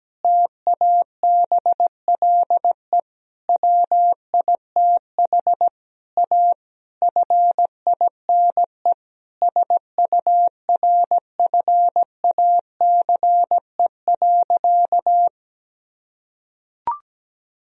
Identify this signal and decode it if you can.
Morse code, heard as a keyed tone, about 17 wpm, 700 Hz. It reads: TABLE WITH A FINE SURFACE.
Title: Sentences from Top 1000 Words - Review Part 3 - 17wpm